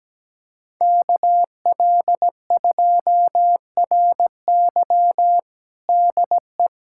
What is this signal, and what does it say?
Morse code, heard as a keyed tone, about 17 wpm, 700 Hz. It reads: KL2RY DE
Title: USA Callsign DE Callsign - 5 Characters - 17wpm